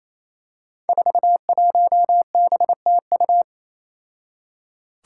Morse code, heard as a keyed tone, about 27 wpm, 700 Hz. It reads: 41BTU
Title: Five Challenging Letter-Number Combinations 28wpm